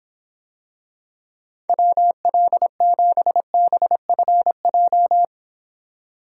Morse code, heard as a keyed tone, about 26 wpm, 700 Hz. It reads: WL7BFJ